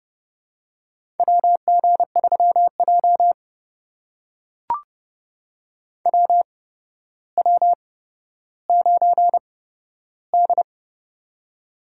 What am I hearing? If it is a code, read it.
Morse code, heard as a keyed tone, about 30 wpm, spaced as for 13 wpm, 700 Hz. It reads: WG3J WW9D
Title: USA Call Signs - 4 Characters - 8x Character Spacing - 30wpm